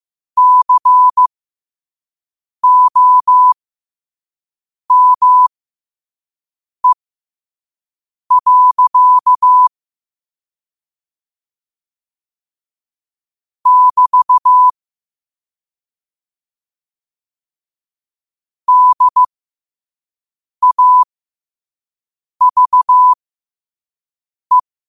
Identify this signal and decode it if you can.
Morse code, heard as a keyed tone, about 15 wpm, spaced as for 5 wpm, 1000 Hz. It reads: COME. <BT> DAVE
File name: qotdfiles / 5wpm